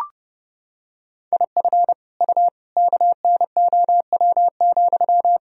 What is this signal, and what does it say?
Morse code, heard as a keyed tone, about 28 wpm, 700 Hz. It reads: IF U KNOW,
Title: Sentences from Top 300 Words and Abbreviations 30wpm